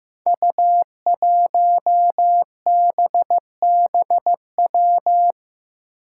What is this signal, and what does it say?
Morse code, heard as a keyed tone, about 15 wpm, 700 Hz. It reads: U1BBW
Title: Five Challenging Letter-Number Combinations 15wpm